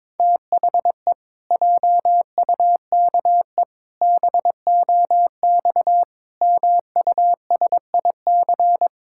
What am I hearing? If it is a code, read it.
Morse code, heard as a keyed tone, about 22 wpm, 700 Hz. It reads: THE JUKE BOX MUSIC
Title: Pangram Challenge v2 - 22wpm